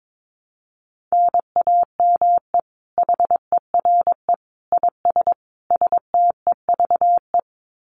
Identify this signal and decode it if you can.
Morse code, heard as a keyed tone, about 22 wpm, 700 Hz. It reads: NAME HERE IS STEVE
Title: Common QSO Phrases 22wpm